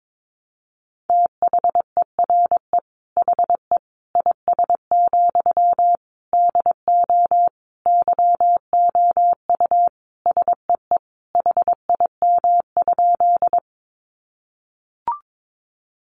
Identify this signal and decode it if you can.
Morse code, heard as a keyed tone, about 22 wpm, 700 Hz. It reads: THERE HE IS, DO YOU SEE HIM?
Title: Sentences from Top 100 Words 22wpm